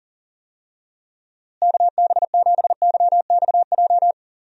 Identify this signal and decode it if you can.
Morse code, heard as a keyed tone, about 40 wpm, 700 Hz. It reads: KB7YXJ
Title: USA Call Signs - 6 Characters - Repeated 2x - 40wpm